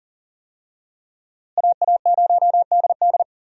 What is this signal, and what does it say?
Morse code, heard as a keyed tone, about 40 wpm, 700 Hz. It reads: AA0DD
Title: USA Call Signs - 5 Characters - Repeated 3x - 40wpm